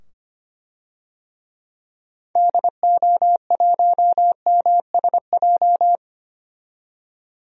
Morse code, heard as a keyed tone, about 25 wpm, 700 Hz. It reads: DO1MSJ
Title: DXCC-100 Call Signs - 6 Characters - 25wpm